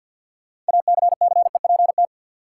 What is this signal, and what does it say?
Morse code, heard as a keyed tone, about 46 wpm, 700 Hz. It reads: ACCEPT
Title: Sentences from Top 1300 Words - New Word Drill - 50wpm